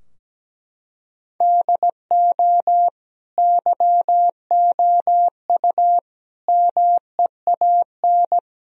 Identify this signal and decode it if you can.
Morse code, heard as a keyed tone, about 17 wpm, 700 Hz. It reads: DO YOU MEAN
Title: Sentences from Top 1000 Words - Review Part 4 - 17wpm